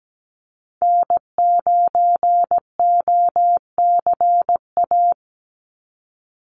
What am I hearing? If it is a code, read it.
Morse code, heard as a keyed tone, about 17 wpm, 700 Hz. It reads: N9OCA
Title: USA Call Signs 17wpm